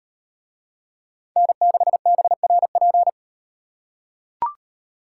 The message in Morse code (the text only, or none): N6BRP